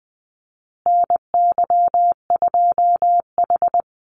Morse code, heard as a keyed tone, about 20 wpm, 700 Hz. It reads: NY2H
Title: Call Signs Repeated 2x - 20wpm